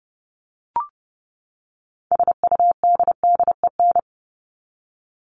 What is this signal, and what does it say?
Morse code, heard as a keyed tone, about 30 wpm, 700 Hz. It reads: SUDDEN
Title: LY Suffix - Top 100 Words with Base Words 30wpm